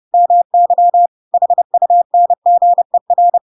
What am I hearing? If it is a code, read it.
Morse code, heard as a keyed tone, about 30 wpm, 700 Hz. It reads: MY HUNGER
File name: Pinocchio / Pinocchio ch7 30 30 ik1hge